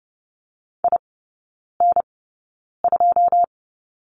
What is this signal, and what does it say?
Morse code, heard as a keyed tone, about 30 wpm, spaced as for 9 wpm, 700 Hz. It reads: IN2
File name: course / 30:10wpm / 087 Lesson 15e - Number 5 - Three Characters 30wpm